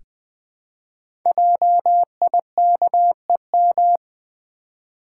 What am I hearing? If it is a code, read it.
Morse code, heard as a keyed tone, about 20 wpm, 700 Hz. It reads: JIKEM
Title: Code Groups - 5 Letters - 20wpm